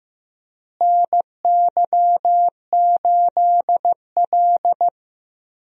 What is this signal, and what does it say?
Morse code, heard as a keyed tone, about 16 wpm, 700 Hz. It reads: NY8L
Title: USA Call Signs - 4 Characters - Repeated 2x - 15wpm